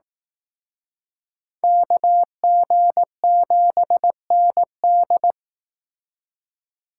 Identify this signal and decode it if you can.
Morse code, heard as a keyed tone, about 18 wpm, 700 Hz. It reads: KG7ND
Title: Speed Racing - USA Call Signs 15wpm